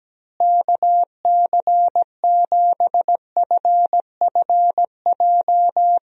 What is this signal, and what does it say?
Morse code, heard as a keyed tone, about 17 wpm, 700 Hz. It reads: KC7FFJ
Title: USA Call Signs - 6 Characters - Repeated 2x - 17wpm